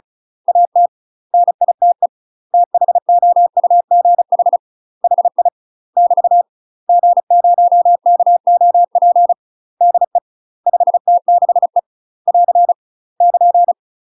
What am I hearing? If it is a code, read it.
Morse code, heard as a keyed tone, about 34 wpm, 700 Hz. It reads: AT NITE THOUGH HI <BT> G0KOP DE 5T6E <AR> <KN>
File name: UK MASTER QSOs TRAINING/35WPM